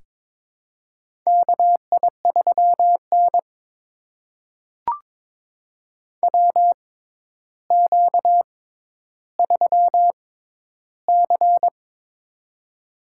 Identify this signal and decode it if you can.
Morse code, heard as a keyed tone, about 22 wpm, spaced as for 11 wpm, 700 Hz. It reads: KI3N WQ3C
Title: USA Call Signs - 4 Characters - 6x Character Spacing - 22wpm